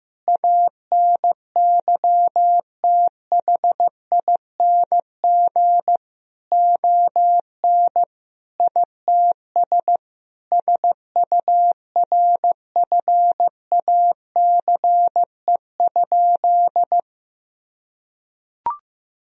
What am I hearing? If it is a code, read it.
Morse code, heard as a keyed tone, about 16 wpm, 700 Hz. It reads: ANYTHING ON ITS SURFACE?
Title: Sentences from Top 1000 Words - Review Part 3 - 15wpm